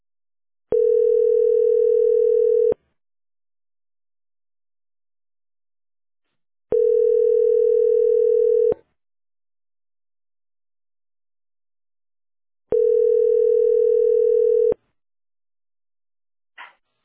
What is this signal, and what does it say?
Ringback tone (North America), 440 + 480 Hz.